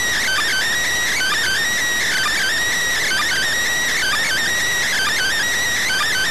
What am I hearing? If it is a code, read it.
Emergency Alert System (SAME) tones (header bursts), heard as harsh buzzing bursts. It is heard from 1.1 to 5.4 s.